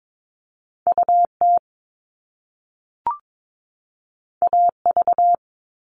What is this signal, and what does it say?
Morse code, heard as a keyed tone, about 22 wpm, 700 Hz. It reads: UT AV
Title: Top Two Letter Combinations in Words 22wpm